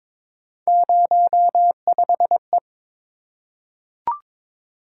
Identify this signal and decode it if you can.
Morse code, heard as a keyed tone, about 22 wpm, 700 Hz. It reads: 05E
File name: practice / 22wpm / Three Letter-Number Combinations - 22wpm